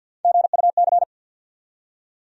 Morse code, heard as a keyed tone, about 47 wpm, 700 Hz. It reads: GUC
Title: USA Call Sign Postfixes - 3 Letters 50wpm